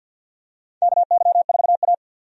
Morse code, heard as a keyed tone, about 47 wpm, 700 Hz. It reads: KY4A
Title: USA Call Signs - 4 Characters - 50wpm